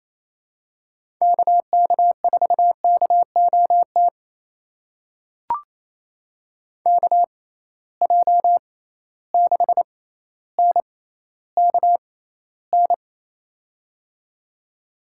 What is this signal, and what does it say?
Morse code, heard as a keyed tone, about 28 wpm, spaced as for 14 wpm, 700 Hz. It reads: KK4KOT KJ6NKN
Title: USA Call Signs - 6 Characters - 6x Character Spacing - 28wpm